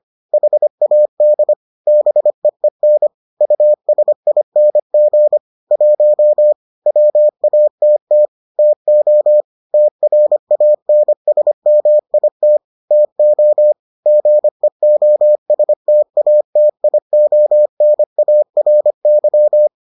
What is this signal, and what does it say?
Morse code, heard as a keyed tone, about 25 wpm, 600 Hz. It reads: HAD BEEN USING 1 WATT TO TRANSMIT TO GEOSTATIONARY